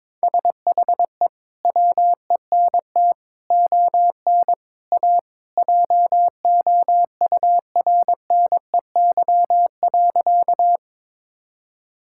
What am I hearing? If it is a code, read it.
Morse code, heard as a keyed tone, about 21 wpm, 700 Hz. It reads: SHE WENT ON A JOURNEY.
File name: practice / 22wpm / Sentences with 2-Word Phrases - 22wpm